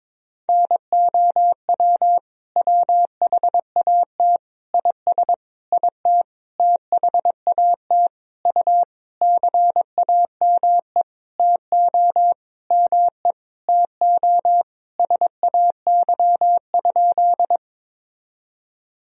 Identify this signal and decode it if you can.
Morse code, heard as a keyed tone, about 22 wpm, 700 Hz. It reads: NOW WHAT IS IT THAT U CAME TO ME TO SAY?
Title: Sentences from Top 200 Words and Abbreviations 22wpm